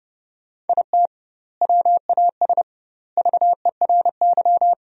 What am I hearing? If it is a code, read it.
Morse code, heard as a keyed tone, about 30 wpm, 700 Hz. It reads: IT WAS VERY
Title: 2x Word-Spacing - Sentences from Top 500 Words - 30wpm